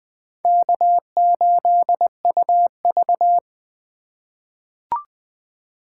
Morse code, heard as a keyed tone, about 21 wpm, 700 Hz. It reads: K8UV
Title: USA Call Signs - 4 Characters - 4x Character Spacing - 20wpm